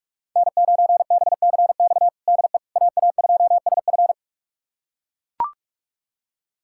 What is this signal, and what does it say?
Morse code, heard as a keyed tone, about 42 wpm, 700 Hz. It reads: N9BCX DE AA2SF
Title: USA Callsign DE Callsign - 5 Characters - 45wpm